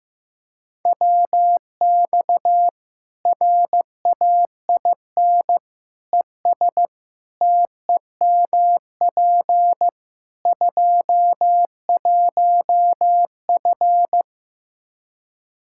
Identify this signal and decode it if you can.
Morse code, heard as a keyed tone, about 15 wpm, 700 Hz. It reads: WX RAIN ES TEMP 21F
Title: Practice QSO - Exchange 2 - Info - 15wpm